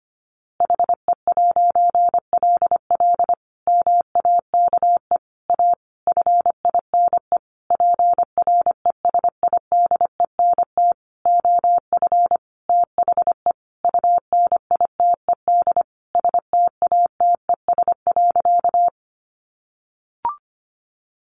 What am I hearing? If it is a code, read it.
Morse code, heard as a keyed tone, about 25 wpm, 700 Hz. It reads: HE'LL MAKE A FINE PRESIDENT OF THE UNITED STATES.
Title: A Sound Of Thunder 25wpm